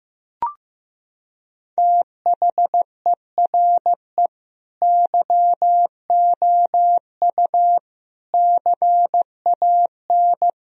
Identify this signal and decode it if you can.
Morse code, heard as a keyed tone, about 15 wpm, 700 Hz. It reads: THERE YOU CAN
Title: Sentences from Top 1000 Words - Review Part 2 - 15wpm